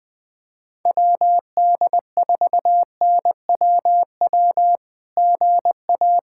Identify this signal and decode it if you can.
Morse code, heard as a keyed tone, about 20 wpm, 700 Hz. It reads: WD4NWW GA
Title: SOTA QSO Practice 20wpm